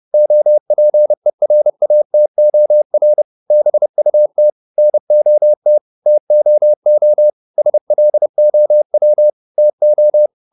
Morse code, heard as a keyed tone, about 30 wpm, 600 Hz. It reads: OPERATOR BUT NOT TOO SLOW TO